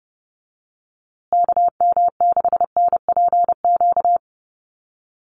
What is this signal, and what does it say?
Morse code, heard as a keyed tone, about 30 wpm, 700 Hz. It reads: KM6NPQ
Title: USA Call Signs Repeated 3x 30wpm